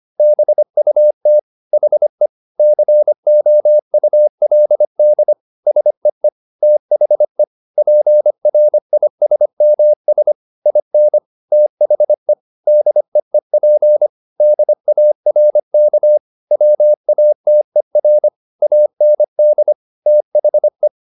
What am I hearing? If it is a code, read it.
Morse code, heard as a keyed tone, about 25 wpm, 600 Hz. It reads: BUT HE COULD SEE THE PRISMS IN THE DEEP DARK WATER AND THE
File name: the old man and the sea part 1 25wpm